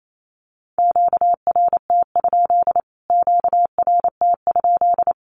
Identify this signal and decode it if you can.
Morse code, heard as a keyed tone, about 28 wpm, 700 Hz. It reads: QRT? QRT?
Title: Q-Codes Repeated 3x 28wpm